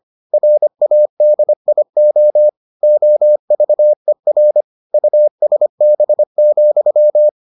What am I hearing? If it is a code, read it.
Morse code, heard as a keyed tone, about 25 wpm, 600 Hz. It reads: RADIO OVER USB,